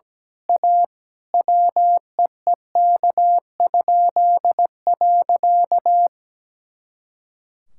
Morse code, heard as a keyed tone, about 17 wpm, 700 Hz. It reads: A WEEK?.
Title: Pangram Challenge v2 - 17wpm